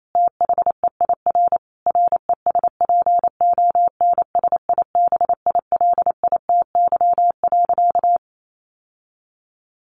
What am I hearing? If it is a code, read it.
Morse code, heard as a keyed tone, about 28 wpm, 700 Hz. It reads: THEIR RESPONSIBILITY.
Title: Sentences from Top 1500 Words 28wpm